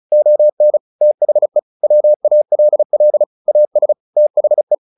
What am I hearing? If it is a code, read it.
Morse code, heard as a keyed tone, about 34 wpm, 600 Hz. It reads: ON THE WALL AS THE
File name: the old man and the sea 2 35wpm